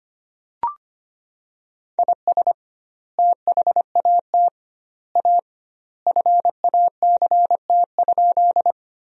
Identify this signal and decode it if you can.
Morse code, heard as a keyed tone, about 25 wpm, 700 Hz. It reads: IS THAT A FACT?
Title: 2x Word-Spacing - Sentences from Top 500 Words - 25wpm